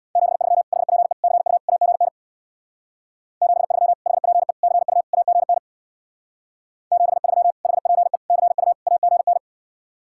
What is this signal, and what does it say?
Morse code, heard as a keyed tone, about 65 wpm, 700 Hz. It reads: 73 HPE CU AGN 73 HPE CU AGN 73 HPE CU AGN